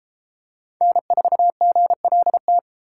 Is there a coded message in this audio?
Yes: Morse code from 0.8 to 2.6 s, heard as a keyed tone, about 32 wpm, 700 Hz. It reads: N4GLT